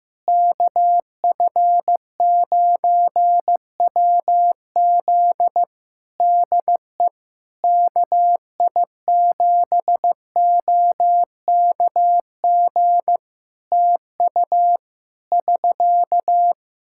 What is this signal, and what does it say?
Morse code, heard as a keyed tone, about 15 wpm, 700 Hz. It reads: KF9WZ DE KI7OKG TU <SK>
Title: K1USN SST QSO Practice 15wpm